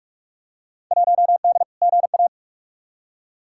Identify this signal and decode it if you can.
Morse code, heard as a keyed tone, about 46 wpm, 700 Hz. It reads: 1D GA
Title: Field Day Exchanges - 45wpm